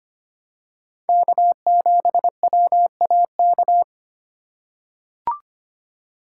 Morse code, heard as a keyed tone, about 25 wpm, 700 Hz. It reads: K7WAK
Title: USA Call Signs - 2x Character Spacing - 25wpm